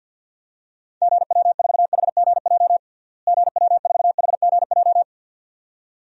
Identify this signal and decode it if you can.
Morse code, heard as a keyed tone, about 51 wpm, 700 Hz. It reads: GW4HGJ GW4HGJ
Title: DXCC-100 Call Signs - 6 Characters - Repeated 2x - 50wpm